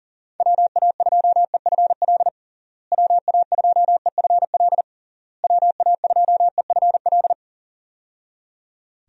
Morse code, heard as a keyed tone, about 40 wpm, 700 Hz. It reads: WA2EFL WA2EFL WA2EFL
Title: USA Call Signs - 6 Characters - Repeated 3x - 40wpm